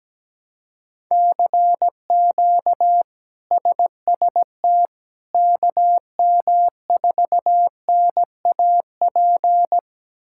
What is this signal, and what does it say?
Morse code, heard as a keyed tone, about 17 wpm, 700 Hz. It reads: CQ SST KM4NAP
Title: K1USN SST QSO Practice - 17wpm